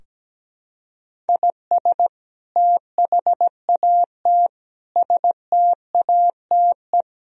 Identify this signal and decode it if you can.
Morse code, heard as a keyed tone, about 17 wpm, 700 Hz. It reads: IS THAT STATE
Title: Sentences from Top 1000 Words - Review Part 3 - 17wpm